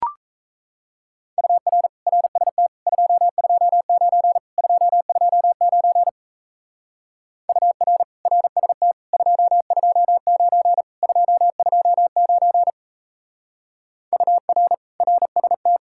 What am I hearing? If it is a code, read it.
Morse code, heard as a keyed tone, about 34 wpm, 700 Hz. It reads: UR RST 229 229 UR RST 229 229 UR RST
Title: Speed-Racing Common QSO Phrases - 28wpm